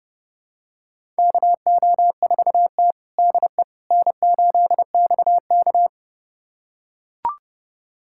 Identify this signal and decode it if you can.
Morse code, heard as a keyed tone, about 32 wpm, 700 Hz. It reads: KO4T DE N8XK
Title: USA Callsign DE Callsign - 4 Characters - 30wpm